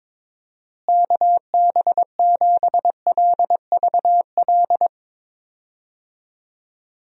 Morse code, heard as a keyed tone, about 22 wpm, 700 Hz. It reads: KB7LVL
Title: USA Call Signs - 6 Characters - Repeated 3x - 22wpm